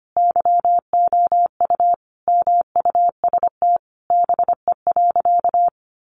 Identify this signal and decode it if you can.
Morse code, heard as a keyed tone, about 25 wpm, 700 Hz. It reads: YOU MUST BE.
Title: Sentences from Top 1400 Words 25wpm